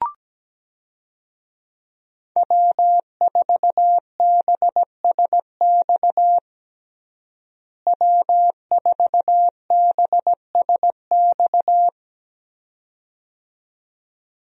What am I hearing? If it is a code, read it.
Morse code, heard as a keyed tone, about 17 wpm, 700 Hz. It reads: W4BSX W4BSX
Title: USA Call Signs - 5 Characters - Repeated 2x - 17wpm